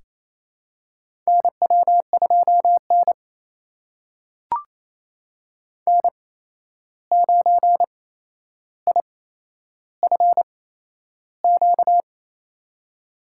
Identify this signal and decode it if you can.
Morse code, heard as a keyed tone, about 28 wpm, spaced as for 11 wpm, 700 Hz. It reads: NW2N N9IFQ
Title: USA Call Signs - 8x Character Spacing - 28wpm